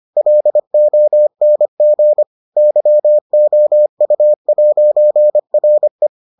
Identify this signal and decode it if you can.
Morse code, heard as a keyed tone, about 25 wpm, 600 Hz. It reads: LONG YOU'RE